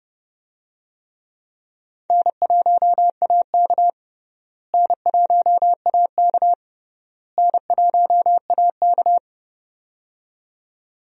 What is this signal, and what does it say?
Morse code, heard as a keyed tone, about 30 wpm, 700 Hz. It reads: N1AK N1AK N1AK
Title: USA Call Signs - 4 Characters - Repeated 3x - 30wpm